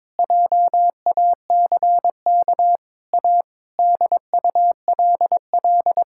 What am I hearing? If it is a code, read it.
Morse code, heard as a keyed tone, about 21 wpm, 700 Hz. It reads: JACK A DULL
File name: practice / 22wpm / Sets of 8 Words - 22wpm